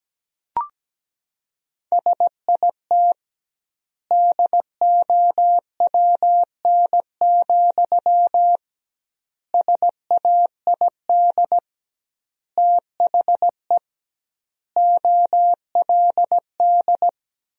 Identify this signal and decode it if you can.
Morse code, heard as a keyed tone, about 17 wpm, 700 Hz. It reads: SIT DOWN, SAID THE OLD
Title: 2x Word-Spacing - Sentences from Top 500 Words - 17wpm